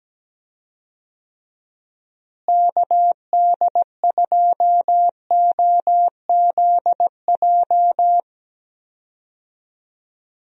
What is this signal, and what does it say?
Morse code, heard as a keyed tone, about 17 wpm, 700 Hz. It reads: KD2OZJ
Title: USA Call Signs - 6 Characters - Repeated 2x - 17wpm